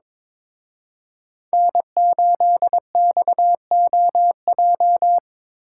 Morse code, heard as a keyed tone, about 27 wpm, 700 Hz. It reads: N8XOJ
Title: Speed Racing - USA 1x3 Call Signs 22wpm